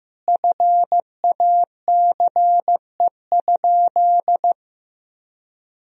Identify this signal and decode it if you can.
Morse code, heard as a keyed tone, about 15 wpm, 700 Hz. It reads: FACE?
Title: Sentences from Top 1000 Words - Review Part 2 - 15wpm